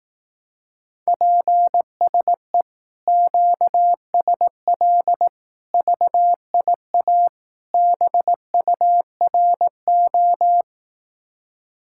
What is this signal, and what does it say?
Morse code, heard as a keyed tone, about 18 wpm, 700 Hz. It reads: PSE QSL VIA BURO
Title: Speed-Racing Common QSO Phrases Encore - 15wpm